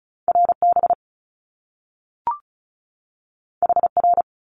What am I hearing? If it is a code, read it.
Morse code, heard as a keyed tone, about 36 wpm, 700 Hz. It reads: RB HR
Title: Top Two Letter Combinations in Words 35wpm